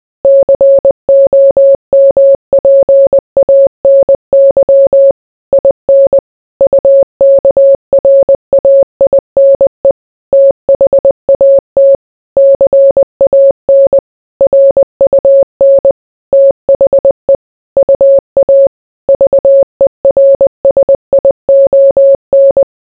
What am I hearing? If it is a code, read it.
Morse code, heard as a keyed tone, about 20 wpm, 550 Hz. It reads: COMPANY IN UKRAINE THAT CAN RUN THE UA VERSION